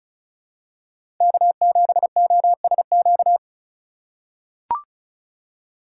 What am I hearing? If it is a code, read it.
Morse code, heard as a keyed tone, about 35 wpm, 700 Hz. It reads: K7OSQ
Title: USA Call Signs - 5 Characters - 3x Character Spacing - 35wpm